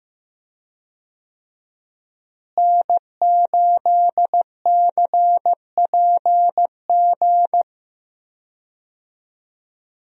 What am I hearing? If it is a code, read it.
Morse code, heard as a keyed tone, about 15 wpm, 700 Hz. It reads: N8CPG
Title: USA Call Signs - 5 Characters - Repeated 3x - 15wpm